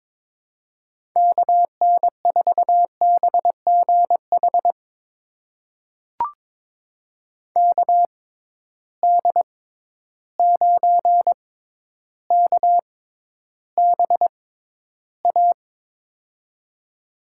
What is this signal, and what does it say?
Morse code, heard as a keyed tone, about 21 wpm, spaced as for 11 wpm, 700 Hz. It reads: KN4BGH KD9KBA
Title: USA Call Signs - 6 Characters - 6x Character Spacing - 22wpm